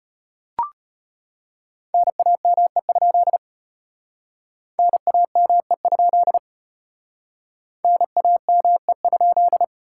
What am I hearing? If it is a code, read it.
Morse code, heard as a keyed tone, about 33 wpm, 700 Hz. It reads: NAME? NAME? NAME?